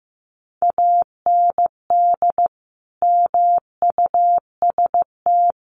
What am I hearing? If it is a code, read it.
Morse code, heard as a keyed tone, about 15 wpm, 700 Hz. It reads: AND MUST
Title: Sentences from Top 1300 Words 15wpm